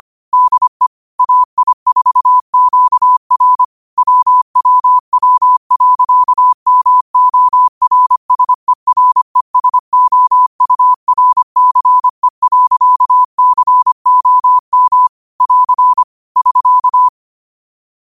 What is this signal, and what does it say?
Morse code, heard as a keyed tone, about 25 wpm, 1000 Hz. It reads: DE AI4QR WWW.MORSERESOURCE.COM <AR> <SK>
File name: qotdfiles / 25wpm